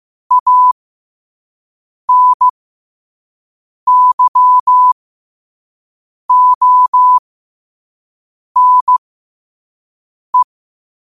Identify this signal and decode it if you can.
Morse code, heard as a keyed tone, about 15 wpm, spaced as for 5 wpm, 1000 Hz. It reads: ANYONE